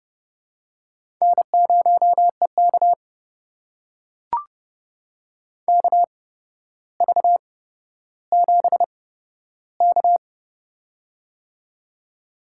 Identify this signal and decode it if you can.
Morse code, heard as a keyed tone, about 30 wpm, spaced as for 13 wpm, 700 Hz. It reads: N0EK KV7K
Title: USA Call Signs - 4 Characters - 8x Character Spacing - 30wpm